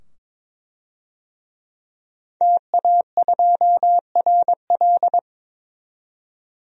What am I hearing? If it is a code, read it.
Morse code, heard as a keyed tone, about 22 wpm, 700 Hz. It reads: TA2RL
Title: DXCC-100 Call Signs - 5 Characters - 22wpm